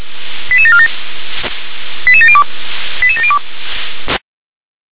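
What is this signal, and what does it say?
Five-tone selective call (ZVEI), heard as quick runs of five tones, tones of about 70 ms. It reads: 88747 88702 88702
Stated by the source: ZVEI five-tone call received on SDR.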